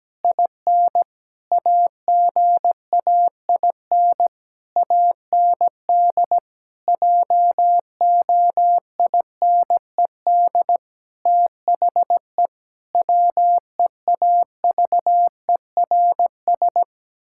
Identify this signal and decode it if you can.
Morse code, heard as a keyed tone, about 17 wpm, 700 Hz. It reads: IN AGAIN AND JOINED THE WEAVERS
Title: Pangram Challenge v2 - 17wpm